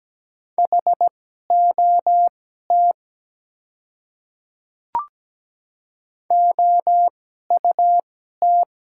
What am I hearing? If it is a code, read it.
Morse code, heard as a keyed tone, about 17 wpm, spaced as for 12 wpm, 700 Hz. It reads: HOT OUT
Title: 2x Character Spacing - Top 100 Words - 17wpm